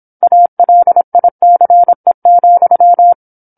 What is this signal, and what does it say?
Morse code, heard as a keyed tone, about 27 wpm, 700 Hz. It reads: ALICE,
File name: 08-Chapter-08 26 WPM